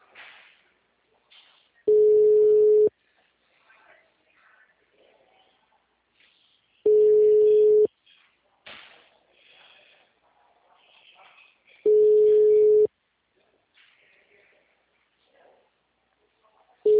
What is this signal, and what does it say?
Ringback tone (Europe, 425 Hz).